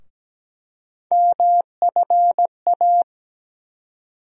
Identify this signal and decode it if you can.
Morse code, heard as a keyed tone, about 17 wpm, 700 Hz. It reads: MFA
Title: USA Call Sign Postfixes - 3 Letters 17wpm